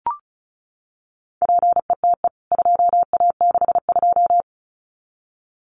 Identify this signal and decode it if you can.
Morse code, heard as a keyed tone, about 34 wpm, 700 Hz. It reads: PETE 2A62
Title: CWT Exchange 35wpm